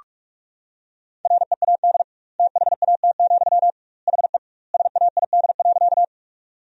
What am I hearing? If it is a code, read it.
Morse code, heard as a keyed tone, about 46 wpm, 700 Hz. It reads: READ THAT, HE SAID.
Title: 2x Word-Spacing - Sentences from Top 200 Words - 45wpm